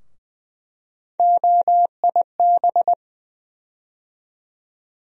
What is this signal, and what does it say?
Morse code, heard as a keyed tone, about 20 wpm, 700 Hz. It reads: OIB
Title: USA Call Sign Postfixes - 3 Letters 20wpm